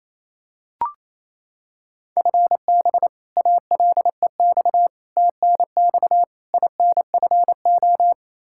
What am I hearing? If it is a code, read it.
Morse code, heard as a keyed tone, about 28 wpm, 700 Hz. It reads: FB ALEX TNX INFO
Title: Practice QSO - Exchange 3 - More Info - 28wpm